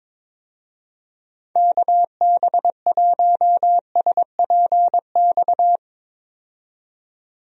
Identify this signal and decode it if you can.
Morse code, heard as a keyed tone, about 22 wpm, 700 Hz. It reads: KB1SPX
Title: USA Call Signs - 6 Characters - 22wpm